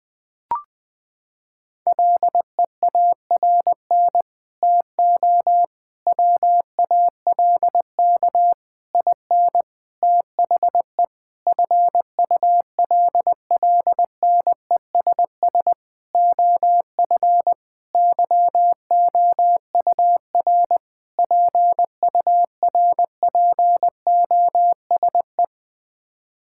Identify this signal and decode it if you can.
Morse code, heard as a keyed tone, about 20 wpm, 700 Hz. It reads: LEARN TO WALK IN THE FULLNESS OF YOUR PURPOSE
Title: Sets of 9 Words - 20wpm